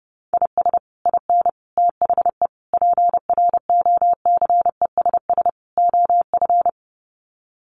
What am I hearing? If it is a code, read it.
Morse code, heard as a keyed tone, about 28 wpm, 700 Hz. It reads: IS IN THE PROCESS OF
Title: Sets of 5 Words Encore2 30wpm